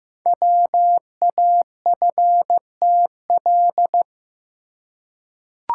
Morse code, heard as a keyed tone, about 15 wpm, 700 Hz. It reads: WAFTL
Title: Code Groups - 5 Letters - 15wpm